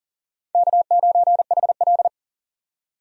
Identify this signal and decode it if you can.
Morse code, heard as a keyed tone, about 40 wpm, 700 Hz. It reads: K9HL
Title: USA Call Signs - 4 Characters - 40wpm